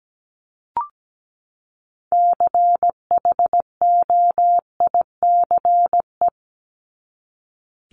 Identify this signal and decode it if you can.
Morse code, heard as a keyed tone, about 17 wpm, 700 Hz. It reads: CHOICE
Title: Top 751-1000 Words 17wpm